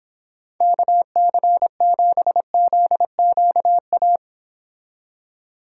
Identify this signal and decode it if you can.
Morse code, heard as a keyed tone, about 26 wpm, 700 Hz. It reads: KC7ZQA